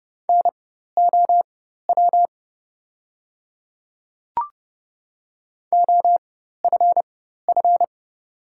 Morse code, heard as a keyed tone, about 30 wpm, spaced as for 14 wpm, 700 Hz. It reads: NOW OFF
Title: 4x Character Spacing - Top 200 Words - 30wpm